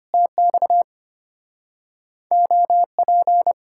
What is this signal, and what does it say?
Morse code, heard as a keyed tone, about 33 wpm, 700 Hz. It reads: TX OP